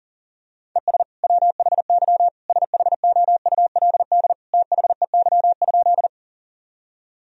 Morse code, heard as a keyed tone, about 40 wpm, 700 Hz. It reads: ES WHY SHOULD THEY?
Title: Sentences from Top 300 Words and Abbreviations 40wpm